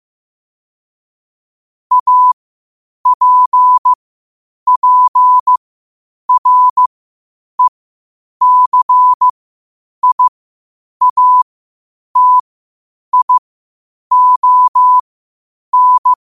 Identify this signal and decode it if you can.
Morse code, heard as a keyed tone, about 15 wpm, spaced as for 8 wpm, 1000 Hz. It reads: APPRECIATION